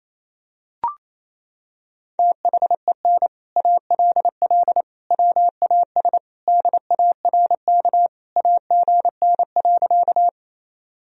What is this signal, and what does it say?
Morse code, heard as a keyed tone, about 28 wpm, 700 Hz. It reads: THEN ALL WAS DARK AGN.
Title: Sentences from Top 500 Words and Abbreviations 28wpm